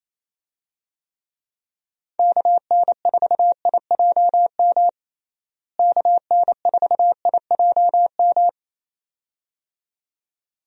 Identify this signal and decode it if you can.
Morse code, heard as a keyed tone, about 28 wpm, 700 Hz. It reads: KN4IJM KN4IJM
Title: USA Call Signs - 6 Characters - Repeated 2x - 28wpm